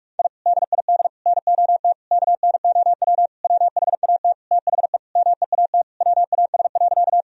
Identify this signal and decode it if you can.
Morse code, heard as a keyed tone, about 46 wpm, 700 Hz. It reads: I DID NOT KNOW WHAT THE MEAT WAS.
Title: Sentences from Top 1000 Words - Review Part 3 - 45wpm